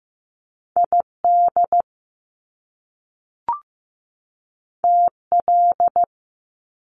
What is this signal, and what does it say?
Morse code, heard as a keyed tone, about 15 wpm, 700 Hz. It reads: ID TL
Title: Top Two Letter Combinations in Words 15wpm